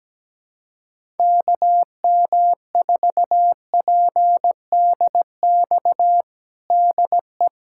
Morse code, heard as a keyed tone, about 17 wpm, 700 Hz. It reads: KM4PDX DE
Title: USA Callsign DE Callsign - 6 Characters - 17wpm